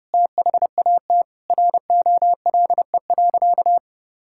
Morse code, heard as a keyed tone, about 30 wpm, 700 Hz. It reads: THAT ROLE.
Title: Sentences from Top 1000 Words - Review Part 6 - 30wpm